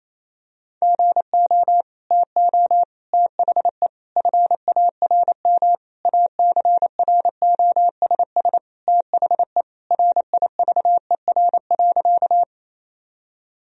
Morse code, heard as a keyed tone, about 28 wpm, 700 Hz. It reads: GO TO THE FARM ACROSS THE RIVER.
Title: Sentences from Top 1000 Words - Review Part 2 - 28wpm